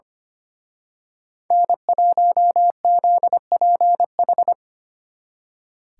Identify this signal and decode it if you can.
Morse code, heard as a keyed tone, about 28 wpm, 700 Hz. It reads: N1ZPH
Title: Speed Racing - USA 1x3 Call Signs 25wpm